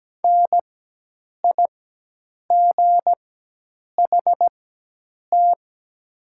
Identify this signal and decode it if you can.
Morse code, heard as a keyed tone, about 17 wpm, spaced as for 8 wpm, 700 Hz. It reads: NIGHT
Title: 4x Character Spacing - Top 300 Words - 17wpm